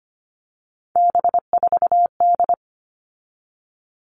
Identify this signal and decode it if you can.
Morse code, heard as a keyed tone, about 25 wpm, 700 Hz. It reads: B4D
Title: All Cut Numbers - 3-Digit 25wpm